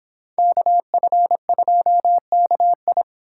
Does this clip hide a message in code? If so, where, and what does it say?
Yes: Morse code from 0.4 to 3.0 s, heard as a keyed tone, about 25 wpm, 700 Hz. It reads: KF2KI